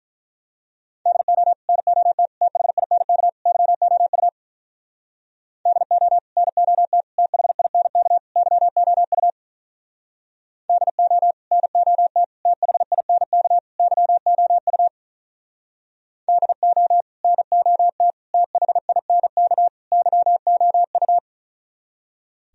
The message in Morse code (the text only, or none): DO NOT THINK YOU DO NOT THINK YOU DO NOT THINK YOU DO NOT THINK YOU